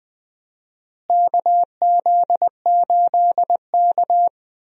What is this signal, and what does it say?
Morse code, heard as a keyed tone, about 20 wpm, 700 Hz. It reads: KZ8K
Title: USA Call Signs - 4 Characters - Repeated 3x - 20wpm